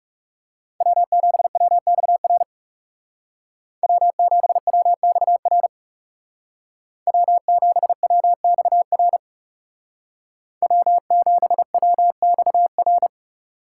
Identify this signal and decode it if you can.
Morse code, heard as a keyed tone, about 35 wpm, 700 Hz. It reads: W7WXR W7WXR W7WXR W7WXR
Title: Speed Racing - USA 1x3 Call Signs 30wpm